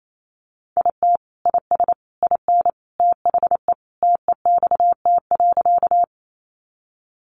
Morse code, heard as a keyed tone, about 28 wpm, 700 Hz. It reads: IT IS IN THE TEXT.
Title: Sentences from Top 1200 Words 28wpm